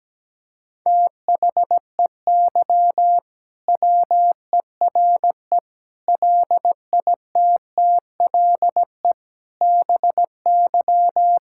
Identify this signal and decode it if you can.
Morse code, heard as a keyed tone, about 17 wpm, 700 Hz. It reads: THEY WERE LITTLE BY
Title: Sentences from Top 1600 Words 17wpm